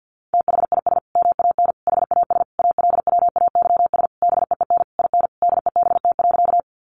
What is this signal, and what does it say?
Morse code, heard as a keyed tone, about 50 wpm, 700 Hz. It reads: THIS MAN HAS ALWAYS BEEN IN DEBT.
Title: Sentences from Top 1300 Words 50wpm